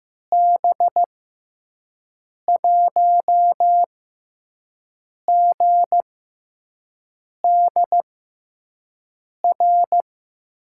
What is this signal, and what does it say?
Morse code, heard as a keyed tone, about 15 wpm, spaced as for 5 wpm, 700 Hz. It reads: B1GDR
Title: USA Call Signs - 6 Characters - 6x Character Spacing - 15wpm